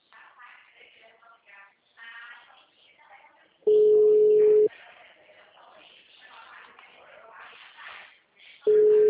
Ringback tone (Europe, 425 Hz).